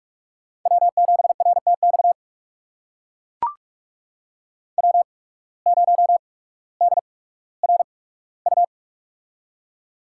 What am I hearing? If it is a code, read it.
Morse code, heard as a keyed tone, about 43 wpm, spaced as for 19 wpm, 700 Hz. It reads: W7RTX W0DRU
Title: USA Call Signs - 5 Characters - 8x Character Spacing - 45wpm